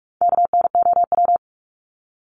Morse code, heard as a keyed tone, about 44 wpm, 700 Hz. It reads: KNOW
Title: Top 300 Words Repeated 45wpm